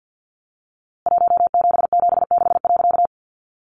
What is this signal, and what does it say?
Morse code, heard as a keyed tone, about 50 wpm, 700 Hz. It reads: 1776.
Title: A Sound Of Thunder 50wpm